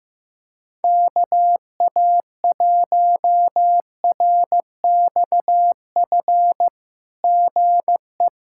Morse code, heard as a keyed tone, about 15 wpm, 700 Hz. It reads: KA1RXF GE
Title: SOTA QSO Practice 15wpm